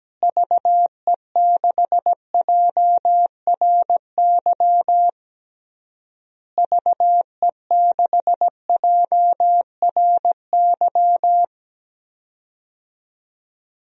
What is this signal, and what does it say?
Morse code, heard as a keyed tone, about 17 wpm, 700 Hz. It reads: VE6JRY VE6JRY
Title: DXCC-100 Call Signs - 6 Characters - Repeated 3x - 17wpm